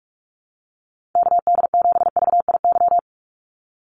Morse code, heard as a keyed tone, about 44 wpm, 700 Hz. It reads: KD7VIY
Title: USA Call Signs 45wpm